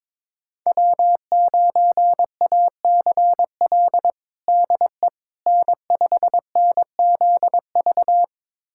Morse code, heard as a keyed tone, about 22 wpm, 700 Hz. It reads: W9ACL DE N5NZV
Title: USA Callsign DE Callsign - 5 Characters - 22wpm